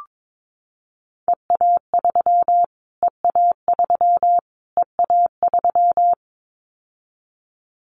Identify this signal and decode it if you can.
Morse code, heard as a keyed tone, about 22 wpm, 700 Hz. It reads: EA3 EA3 EA3
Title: Callsign Prefixes for Top 25 DXCC - Repeated 3x - 22wpm